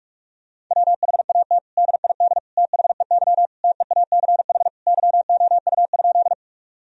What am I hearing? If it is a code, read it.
Morse code, heard as a keyed tone, about 41 wpm, 700 Hz. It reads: WHAT DID THEY TEACH YOU?
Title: Sentences from Top 1000 Words - Review Part 2 - 45wpm